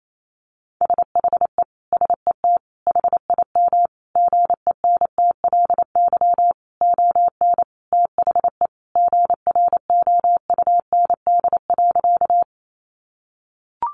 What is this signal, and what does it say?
Morse code, heard as a keyed tone, about 28 wpm, 700 Hz. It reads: SHE SET HIM GENTLY ON THE GROUND.
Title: Sentences from Top 1500 Words 28wpm